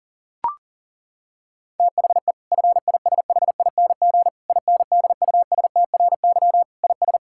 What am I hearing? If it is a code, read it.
Morse code, heard as a keyed tone, about 37 wpm, 700 Hz. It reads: THE FISHING INDUSTRY IS
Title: Sentences from Top 1000 Words - Review Part 6 - 40wpm